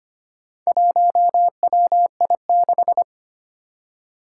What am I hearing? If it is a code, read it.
Morse code, heard as a keyed tone, about 25 wpm, 700 Hz. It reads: 1WI6
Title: Four Challenging Letter-Number Combinations 25wpm